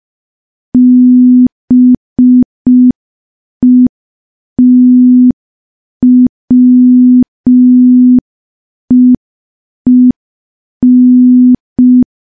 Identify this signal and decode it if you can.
Morse code, heard as a keyed tone, about 5 wpm, 250 Hz. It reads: BETWEEN
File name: morse / morse-training-05WPM